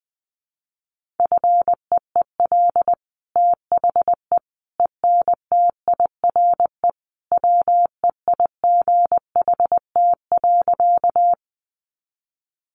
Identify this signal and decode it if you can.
Morse code, heard as a keyed tone, about 20 wpm, 700 Hz. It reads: FEEL THE ENTIRE WEIGHT.
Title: Sentences from Top 800 Words 20wpm